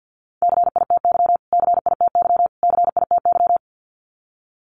Morse code, heard as a keyed tone, about 48 wpm, 700 Hz. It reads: CITY CITY CITY
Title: Top 300 Words Repeated 50wpm